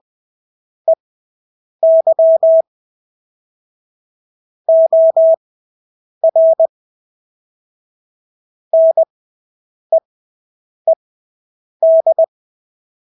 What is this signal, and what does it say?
Morse code, heard as a keyed tone, about 20 wpm, spaced as for 8 wpm, 650 Hz. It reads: EY OR NEED
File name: Engelske ord 20/8 wpm